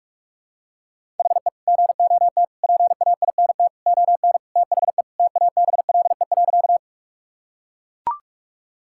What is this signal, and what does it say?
Morse code, heard as a keyed tone, about 43 wpm, 700 Hz. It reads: HE GOT PAINT ON THE TABLE.
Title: Sentences from Top 1000 Words - Review Part 6 - 45wpm